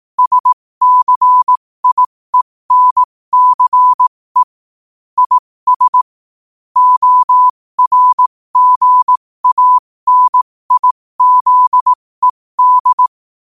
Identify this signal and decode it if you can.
Morse code, heard as a keyed tone, about 18 wpm, spaced as for 15 wpm, 1000 Hz. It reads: SCIENCE IS ORGANIZED
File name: qotdfiles / 15wpm